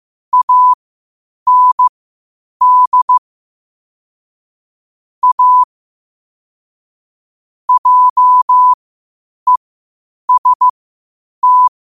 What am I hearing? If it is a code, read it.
Morse code, heard as a keyed tone, about 15 wpm, spaced as for 8 wpm, 1000 Hz. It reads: AND A JEST
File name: qotdfiles / 8wpm